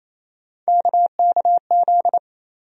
Morse code, heard as a keyed tone, about 28 wpm, 700 Hz. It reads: KKZ
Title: USA Call Sign Postfixes - 3 Letters 28wpm